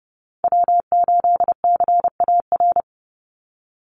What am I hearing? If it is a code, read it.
Morse code, heard as a keyed tone, about 30 wpm, 700 Hz. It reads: W8CAR
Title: Call Signs - 30wpm